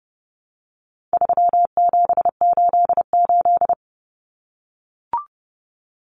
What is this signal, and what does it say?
Morse code, heard as a keyed tone, about 30 wpm, 700 Hz. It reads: 3788
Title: Common Cut Numbers - 4-Digit 30wpm